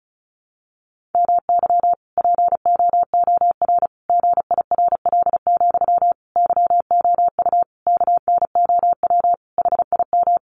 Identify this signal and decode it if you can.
Morse code, heard as a keyed tone, about 34 wpm, 700 Hz. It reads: MY POOR GIRL, YOU KNOW HIM
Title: Sentences from Top 1000 Words 35wpm